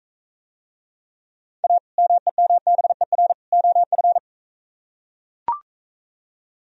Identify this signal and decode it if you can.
Morse code, heard as a keyed tone, about 42 wpm, 700 Hz. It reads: A MEMBER OF